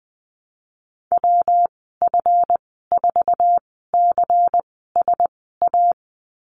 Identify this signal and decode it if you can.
Morse code, heard as a keyed tone, about 19 wpm, spaced as for 14 wpm, 700 Hz. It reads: WF4CSA
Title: 186 Lesson 30g - Letter b - Call Signs 20wpm